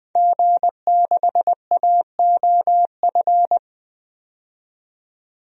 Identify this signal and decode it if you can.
Morse code, heard as a keyed tone, about 20 wpm, 700 Hz. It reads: G6AOF
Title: DXCC-100 Call Signs - 5 Characters - 20wpm